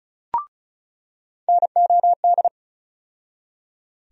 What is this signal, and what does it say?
Morse code, heard as a keyed tone, about 36 wpm, 700 Hz. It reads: NOD
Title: USA Call Sign Postfixes - 3 Letters 35wpm